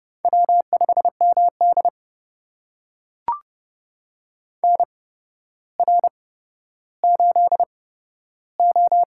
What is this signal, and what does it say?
Morse code, heard as a keyed tone, about 30 wpm, spaced as for 13 wpm, 700 Hz. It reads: W5MD NR8O
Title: USA Call Signs - 4 Characters - 8x Character Spacing - 30wpm